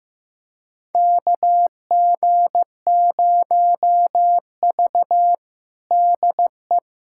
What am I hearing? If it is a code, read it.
Morse code, heard as a keyed tone, about 15 wpm, 700 Hz. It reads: KG0V DE
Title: USA Callsign DE Callsign - 4 Characters - 15wpm